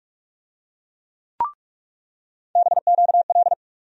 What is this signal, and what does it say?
Morse code, heard as a keyed tone, about 42 wpm, 700 Hz. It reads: BQL
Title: USA Call Sign Postfixes - 3 Letters 45wpm